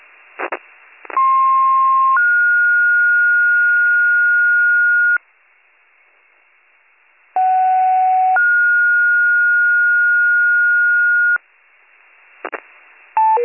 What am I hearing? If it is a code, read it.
Two-tone page, heard as a one-second tone, then a three-second tone, about 1060 then 1430 Hz, about 750 then 1430 Hz. It is heard from 1.2 to 11.4 s.